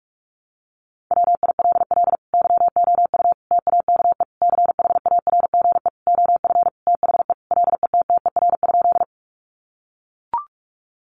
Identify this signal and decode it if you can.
Morse code, heard as a keyed tone, about 44 wpm, 700 Hz. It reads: WILL YOU TAKE CHARGE OF THE LETTER?
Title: Sentences from Top 900 Words 45wpm